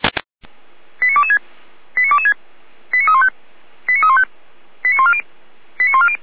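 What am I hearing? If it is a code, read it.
Five-tone selective call (ZVEI), heard as quick runs of five tones, tones of about 70 ms. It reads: 89226 89226 89315 89315 89140 89140
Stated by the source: ZVEI five-tone selective call received via SDR.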